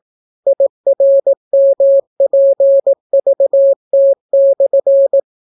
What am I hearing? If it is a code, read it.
Morse code, heard as a keyed tone, about 18 wpm, 550 Hz. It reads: IRMPVT/